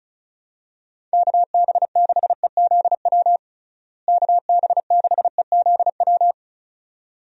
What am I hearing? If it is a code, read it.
Morse code, heard as a keyed tone, about 36 wpm, 700 Hz. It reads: KB6EZW KB6EZW
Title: USA Call Signs - 6 Characters - Repeated 2x - 35wpm